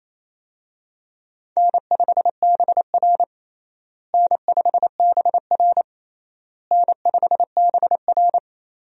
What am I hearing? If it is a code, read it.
Morse code, heard as a keyed tone, about 28 wpm, 700 Hz. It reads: N5BR N5BR N5BR
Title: USA Call Signs - 4 Characters - Repeated 3x - 28wpm